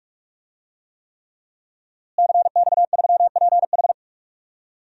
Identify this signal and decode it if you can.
Morse code, heard as a keyed tone, about 45 wpm, 700 Hz. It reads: CX3PH
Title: DXCC-100 Call Signs - 5 Characters - Repeated 3x - 45wpm